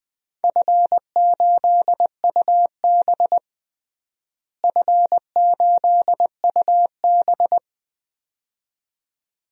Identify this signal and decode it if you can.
Morse code, heard as a keyed tone, about 19 wpm, 700 Hz. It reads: F8UB F8UB
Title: DXCC-100 Call Signs - 4 Characters - Repeated 2x - 20wpm